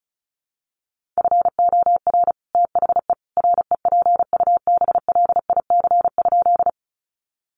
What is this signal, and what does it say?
Morse code, heard as a keyed tone, about 34 wpm, 700 Hz. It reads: FOR THE REPUBLIC?
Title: Sentences from Top 1500 Words 35wpm